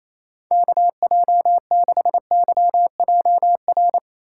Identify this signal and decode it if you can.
Morse code, heard as a keyed tone, about 26 wpm, 700 Hz. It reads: KJ6YJR
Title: Speed Racing - USA 2x3 Call Signs 28wpm